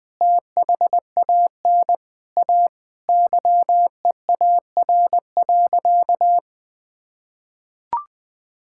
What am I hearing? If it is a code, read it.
Morse code, heard as a keyed tone, about 21 wpm, 700 Hz. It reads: THAN A YEAR.
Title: Sentences with 4-Word Phrases - 20wpm